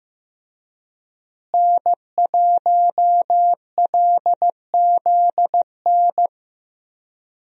Morse code, heard as a keyed tone, about 15 wpm, 700 Hz. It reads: N1LZN
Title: USA Call Signs - 5 Characters - Repeated 3x - 15wpm